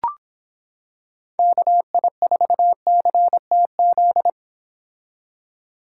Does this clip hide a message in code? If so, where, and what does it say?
Yes: Morse code from 1.4 to 4.3 s, heard as a keyed tone, about 26 wpm, 700 Hz. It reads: KI4CTZ